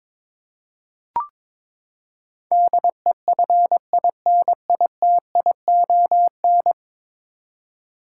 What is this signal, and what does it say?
Morse code, heard as a keyed tone, about 22 wpm, 700 Hz. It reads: DEFINITION
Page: morse-code-ninja-practice-22wpm